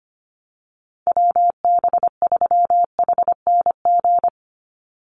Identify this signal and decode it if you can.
Morse code, heard as a keyed tone, about 25 wpm, 700 Hz. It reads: WB3HNG
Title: USA Call Signs 25wpm